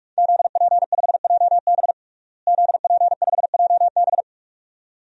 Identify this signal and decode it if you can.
Morse code, heard as a keyed tone, about 42 wpm, 700 Hz. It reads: ZP5JB ZP5JB
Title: DXCC-100 Call Signs - 5 Characters - Repeated 2x - 45wpm